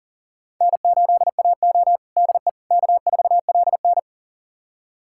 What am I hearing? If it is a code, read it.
Morse code, heard as a keyed tone, about 39 wpm, 700 Hz. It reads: N8AO DE K4LN
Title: USA Callsign DE Callsign - 4 Characters - 40wpm